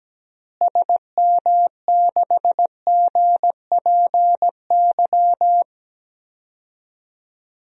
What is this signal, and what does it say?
Morse code, heard as a keyed tone, about 17 wpm, 700 Hz. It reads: SM6GPY
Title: DXCC-100 Call Signs - 6 Characters - Repeated 3x - 17wpm